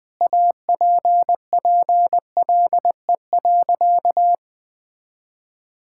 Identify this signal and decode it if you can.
Morse code, heard as a keyed tone, about 21 wpm, 700 Hz. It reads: APPLE.
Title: Sentences with 2-Word Phrases - 20wpm